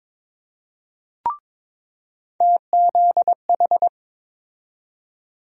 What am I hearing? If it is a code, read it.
Morse code, heard as a keyed tone, about 22 wpm, 700 Hz. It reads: TZH
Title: USA Call Sign Postfixes - 3 Letters 22wpm